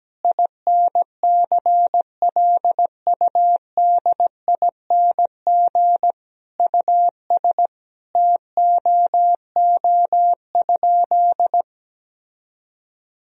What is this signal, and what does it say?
Morse code, heard as a keyed tone, about 17 wpm, 700 Hz. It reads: INCLUDING US TOO?
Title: Sentences from Top 1000 Words - Review Part 4 - 17wpm